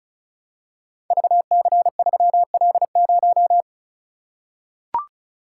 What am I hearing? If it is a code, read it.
Morse code, heard as a keyed tone, about 36 wpm, 700 Hz. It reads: VC3L0
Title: Code Groups - 5 Letter:Numbers - 35wpm